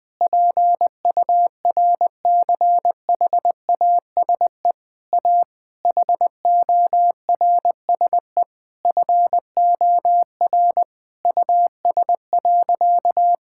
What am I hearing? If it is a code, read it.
Morse code, heard as a keyed tone, about 20 wpm, 700 Hz. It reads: PURCHASE A HORSE FOR US.
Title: Sentences from Top 1000 Words - Review Part 5 - 20wpm